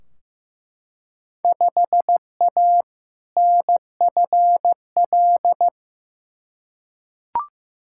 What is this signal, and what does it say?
Morse code, heard as a keyed tone, about 15 wpm, 700 Hz. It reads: 5A NFL
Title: Field Day Exchanges - 15wpm